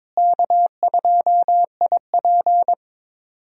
Morse code, heard as a keyed tone, about 22 wpm, 700 Hz. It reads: K2IP
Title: USA Call Signs - 4 Characters - Repeated 3x - 22wpm